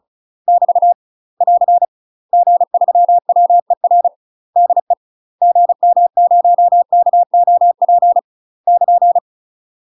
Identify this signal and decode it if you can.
Morse code, heard as a keyed tone, about 35 wpm, 700 Hz. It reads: <BT> <AR> G3WER DE GM0KOP <KN>
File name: UK MASTER QSOs TRAINING / 35WPM / UK MASTER QSOs 0037